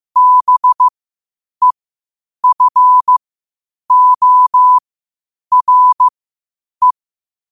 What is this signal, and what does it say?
Morse code, heard as a keyed tone, about 15 wpm, spaced as for 8 wpm, 1000 Hz. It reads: BEFORE
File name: qotdfiles / 8wpm